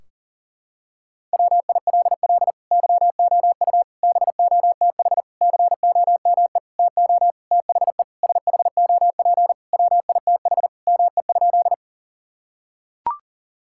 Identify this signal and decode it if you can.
Morse code, heard as a keyed tone, about 40 wpm, 700 Hz. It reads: WILL YOU BOTH COME TO THE SHOP WITH ME?
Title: Sentences from Top 1600 Words 40wpm